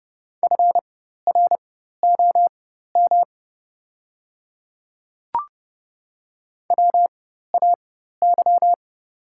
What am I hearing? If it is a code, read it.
Morse code, heard as a keyed tone, about 30 wpm, spaced as for 14 wpm, 700 Hz. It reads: FROM WAY
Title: 4x Character Spacing - Top 100 Words - 30wpm